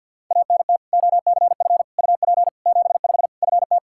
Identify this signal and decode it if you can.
Morse code, heard as a keyed tone, about 50 wpm, 700 Hz. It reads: ANT OCF UP 75 FT
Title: Practice QSO - Exchange 2 - Info - 50wpm